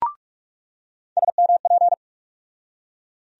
Morse code, heard as a keyed tone, about 41 wpm, 700 Hz. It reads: SMP
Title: USA Call Sign Postfixes - 3 Letters 45wpm